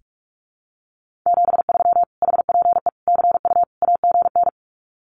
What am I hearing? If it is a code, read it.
Morse code, heard as a keyed tone, about 46 wpm, 700 Hz. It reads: 73 HPE CU AGN
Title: Common QSO Phrases Encore 45wpm